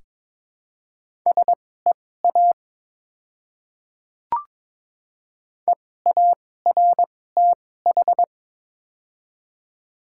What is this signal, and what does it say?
Morse code, heard as a keyed tone, about 22 wpm, spaced as for 16 wpm, 700 Hz. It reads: SEA EARTH
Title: 2x Character Spacing - Top 300 Words - 22wpm